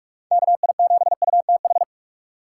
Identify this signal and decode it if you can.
Morse code, heard as a keyed tone, about 46 wpm, 700 Hz. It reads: KI7UTH
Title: USA Call Signs - 2x Character Spacing - 45wpm